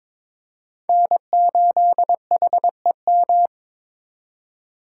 Morse code, heard as a keyed tone, about 26 wpm, 700 Hz. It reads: N8HEM
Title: Speed Racing - USA Call Signs 22wpm